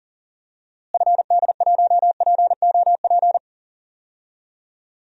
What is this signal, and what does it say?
Morse code, heard as a keyed tone, about 40 wpm, 700 Hz. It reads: FD1POP
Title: DXCC-100 Call Signs - 6 Characters - Repeated 2x - 40wpm